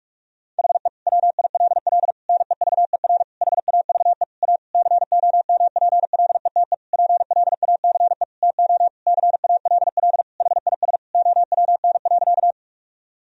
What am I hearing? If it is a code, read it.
Morse code, heard as a keyed tone, about 45 wpm, 700 Hz. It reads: HE WILL NEVER HAVE A COMPLETE PLACE TO CALL HIS OWN.
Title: Sentences from Top 1000 Words - Review Part 2 - 45wpm